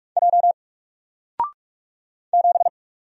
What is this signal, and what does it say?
Morse code, heard as a keyed tone, about 44 wpm, 700 Hz. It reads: J 7
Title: Single Letter-Number 45wpm